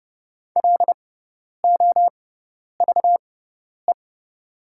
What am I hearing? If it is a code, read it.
Morse code, heard as a keyed tone, about 30 wpm, spaced as for 10 wpm, 690 Hz. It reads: LOVE
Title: 6x Character Spacing - Top 500 Words - 30wpm